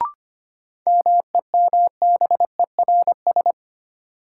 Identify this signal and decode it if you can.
Morse code, heard as a keyed tone, about 25 wpm, 700 Hz. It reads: MEMBERS
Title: Sets of 3 Words - New Word Drill - 25wpm